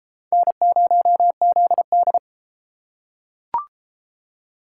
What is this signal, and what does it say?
Morse code, heard as a keyed tone, about 30 wpm, 700 Hz. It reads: N0ZD